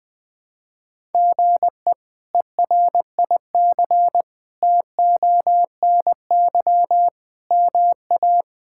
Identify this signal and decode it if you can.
Morse code, heard as a keyed tone, about 20 wpm, 700 Hz. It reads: GE ERIC TONY MA
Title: K1USN SST QSO Practice - 20wpm